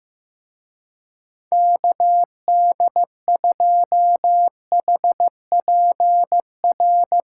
Morse code, heard as a keyed tone, about 16 wpm, 700 Hz. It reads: KD2HPR
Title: Speed Racing - USA 2x3 Call Signs 15wpm